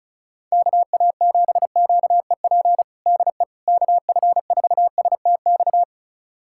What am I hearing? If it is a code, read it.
Morse code, heard as a keyed tone, about 36 wpm, 700 Hz. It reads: KA7QEP DE KF4STX
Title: USA Callsign DE Callsign 35wpm